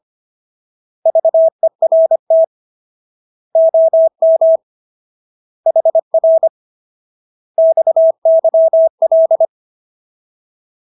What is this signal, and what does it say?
Morse code, heard as a keyed tone, about 25 wpm, 650 Hz. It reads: VERT OM HR XYL